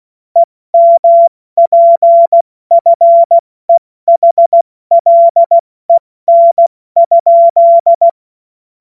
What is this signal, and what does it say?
Morse code, heard as a keyed tone, about 16 wpm, 680 Hz. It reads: EMPFEHLEN?
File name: Morsekurs / Witze / Witz 20. Dezember 2022 70 F80